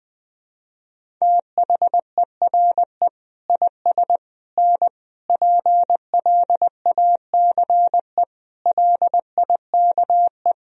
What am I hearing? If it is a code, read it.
Morse code, heard as a keyed tone, about 20 wpm, 700 Hz. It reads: THERE IS N PLACE LIKE
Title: Sentences from Top 200 Words and Abbreviations 20wpm